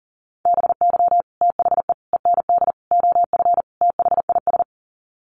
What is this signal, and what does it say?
Morse code, heard as a keyed tone, about 40 wpm, 700 Hz. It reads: BY THE END OF THIS
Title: Sets of 5 Words Encore 40wpm